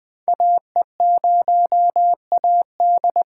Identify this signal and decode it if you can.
Morse code, heard as a keyed tone, about 20 wpm, 700 Hz. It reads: AE0AD
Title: Code Groups - 5 Letter:Numbers - 20wpm